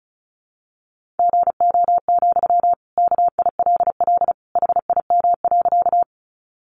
Morse code, heard as a keyed tone, about 34 wpm, 700 Hz. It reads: GO, KILL HIM.
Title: Sentences from Top 1400 Words 35wpm